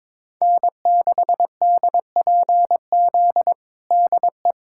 Morse code, heard as a keyed tone, about 22 wpm, 700 Hz. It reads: N6DPZ DE
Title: USA Callsign DE Callsign - 5 Characters - 22wpm